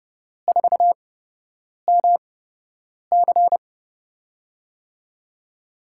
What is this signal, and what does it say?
Morse code, heard as a keyed tone, about 30 wpm, spaced as for 8 wpm, 700 Hz. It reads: 4MC
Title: USA Call Signs - 8x Character Spacing - 30wpm